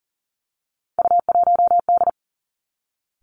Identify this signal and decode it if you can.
Morse code, heard as a keyed tone, about 38 wpm, 700 Hz. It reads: U1D
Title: Three Challenging Letter-Number Combinations 40wpm